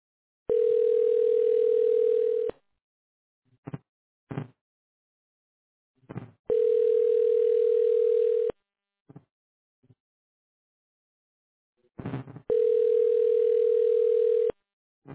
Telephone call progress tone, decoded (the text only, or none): ringback tone